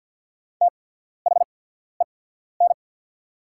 Morse code, heard as a keyed tone, about 48 wpm, spaced as for 14 wpm, 700 Hz. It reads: THEN
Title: 8x Character Spacing - Top 100 Words - 50wpm